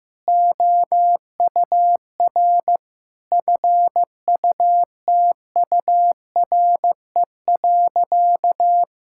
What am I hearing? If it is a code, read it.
Morse code, heard as a keyed tone, about 15 wpm, 700 Hz. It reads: OUR FUTURE.
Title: Sentences from Top 1000 Words - Review Part 5 - 15wpm